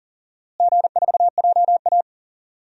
Morse code, heard as a keyed tone, about 40 wpm, 700 Hz. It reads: G4JA
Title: DXCC-100 Call Signs - 4 Characters - Repeated 3x - 40wpm